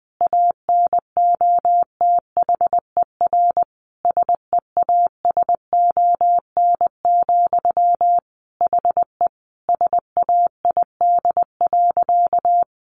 Morse code, heard as a keyed tone, about 20 wpm, 700 Hz. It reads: ANOTHER SEASON, HE SAID.
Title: Sentences from Top 1400 Words 20wpm